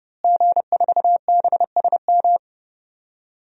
Morse code, heard as a keyed tone, about 30 wpm, 700 Hz. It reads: G4BSM
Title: DXCC-100 Call Signs - 5 Characters - Repeated 3x - 30wpm